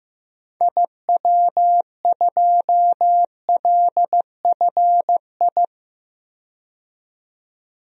Morse code, heard as a keyed tone, about 15 wpm, 700 Hz. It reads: IW2LFI
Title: DXCC-100 Call Signs - 6 Characters - 15wpm